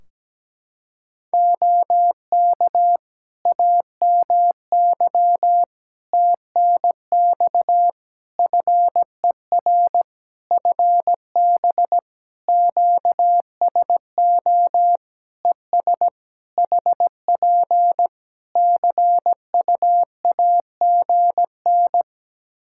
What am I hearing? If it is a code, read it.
Morse code, heard as a keyed tone, about 17 wpm, 700 Hz. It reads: OK AMY TNX FER FB QSO ES HP CUAGN
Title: Practice QSO - Exchange 4 - Ending - 17wpm